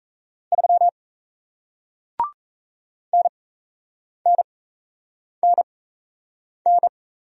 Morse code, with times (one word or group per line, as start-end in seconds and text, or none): 0.5-0.9 3
3.1-6.9 NNNN